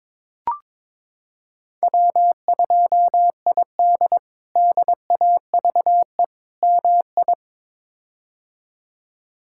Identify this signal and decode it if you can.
Morse code, heard as a keyed tone, about 22 wpm, 700 Hz. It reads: W2ID DAVE MI
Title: K1USN SST QSO Practice - 22wpm